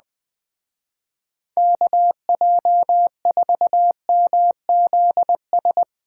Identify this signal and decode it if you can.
Morse code, heard as a keyed tone, about 22 wpm, 700 Hz. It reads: KJ4MZS
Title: Speed Racing - USA Call Signs 20wpm